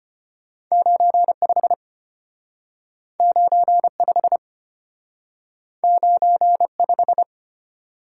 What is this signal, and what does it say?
Morse code, heard as a keyed tone, about 34 wpm, 700 Hz. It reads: 95 95 95